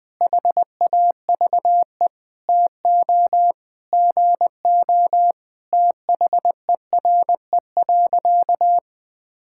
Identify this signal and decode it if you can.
Morse code, heard as a keyed tone, about 20 wpm, 700 Hz. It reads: HAVE TO GO THERE.
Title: Sentences with 5-Word Phrases - 20wpm